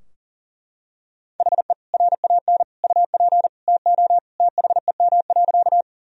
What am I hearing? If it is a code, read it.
Morse code, heard as a keyed tone, about 40 wpm, 700 Hz. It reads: HE RAN UP TO THEM.
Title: Sentences from Top 1000 Words - Review Part 3 - 40wpm